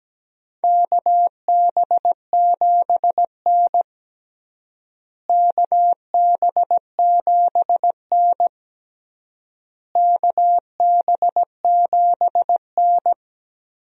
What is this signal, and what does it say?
Morse code, heard as a keyed tone, about 17 wpm, 700 Hz. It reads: KB7N KB7N KB7N
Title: USA Call Signs - 4 Characters - Repeated 3x - 17wpm